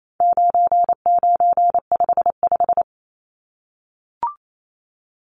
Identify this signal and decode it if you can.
Morse code, heard as a keyed tone, about 28 wpm, 700 Hz. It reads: 9955
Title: Numbers - 4-Digit 28wpm